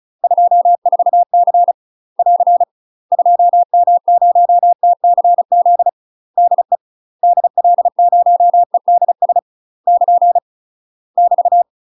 Morse code, heard as a keyed tone, about 36 wpm, 700 Hz. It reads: 24C <AR> 2M0TCZ DE DL0EDS <KN> <BT>
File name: UK MASTER QSOs TRAINING / 35WPM / UK MASTER QSOs 0005